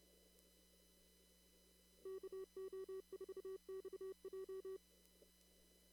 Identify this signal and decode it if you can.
Morse code, heard as a keyed tone, about 30 wpm, 380 Hz. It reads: KO4XJ
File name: CW / kt5j wpx cw-op-k5pi